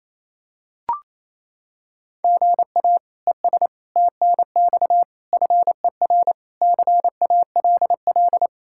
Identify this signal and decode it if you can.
Morse code, heard as a keyed tone, about 27 wpm, 700 Hz. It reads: GA ES TNX FER CALL
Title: K1USN SST QSO Practice 28wpm